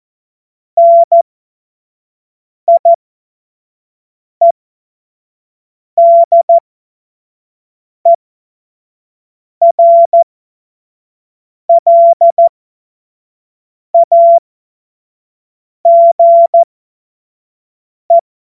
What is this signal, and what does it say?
Morse code, heard as a keyed tone, about 14 wpm, spaced as for 5 wpm, 680 Hz. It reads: NIEDERLAGE
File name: Morsekurs / Witze / Witz 02. Februar 2021, 25 F70